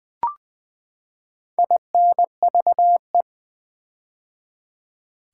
Morse code, heard as a keyed tone, about 20 wpm, 700 Hz. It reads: INVE
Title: Build A Word - 4 Letters At A Time - 12-Letter Words - 20wpm